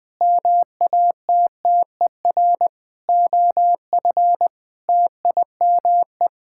Morse code, heard as a keyed tone, about 20 wpm, 700 Hz. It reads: MATTER OF TIME
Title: Sentences with 6-Word Phrases - 20wpm